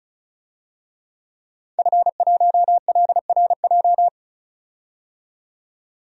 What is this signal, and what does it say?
Morse code, heard as a keyed tone, about 36 wpm, 700 Hz. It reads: F1LRJ
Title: DXCC-100 Call Signs - 5 Characters - 35wpm